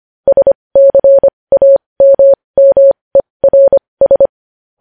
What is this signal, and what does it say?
Morse code, heard as a keyed tone, about 25 wpm, spaced as for 20 wpm, 550 Hz. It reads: SCAMMERS
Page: news-headlines-in-morse-code-at-20-wpm